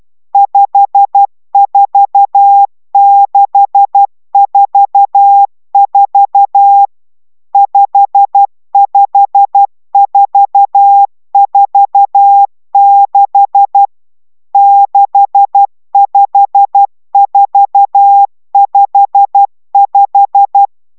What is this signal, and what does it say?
Morse code, heard as a keyed tone, about 12 wpm, 800 Hz. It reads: 54644 55446 65455
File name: taviro tanulas / cw 69 20 10